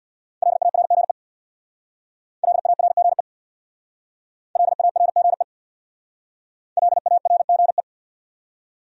Morse code, heard as a keyed tone, about 61 wpm, 700 Hz. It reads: LARGE LARGE LARGE LARGE
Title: Speed-Racing Top 200 Words 50wpm